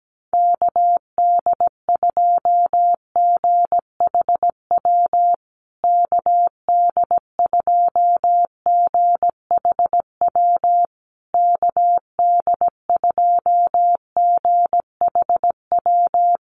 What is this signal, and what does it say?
Morse code, heard as a keyed tone, about 17 wpm, 700 Hz. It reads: KD2GHW KD2GHW KD2GHW
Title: USA Call Signs Repeated 3x 17wpm